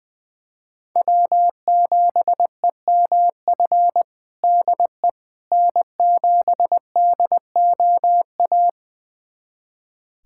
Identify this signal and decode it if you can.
Morse code, heard as a keyed tone, about 20 wpm, 700 Hz. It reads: W7EMF DE N7DOA
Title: USA Callsign DE Callsign - 5 Characters - 20wpm